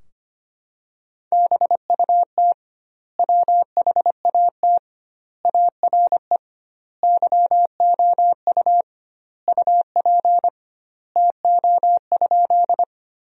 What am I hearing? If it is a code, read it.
Morse code, heard as a keyed tone, about 25 wpm, 700 Hz. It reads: BUT WHAT ARE YOU UP TO?
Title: 2x Word-Spacing - Sentences from Top 100 Words - 25wpm